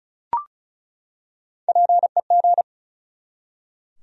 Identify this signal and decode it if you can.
Morse code, heard as a keyed tone, about 34 wpm, 700 Hz. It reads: PEG
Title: USA Common Names - Max 4 Letters - 35wpm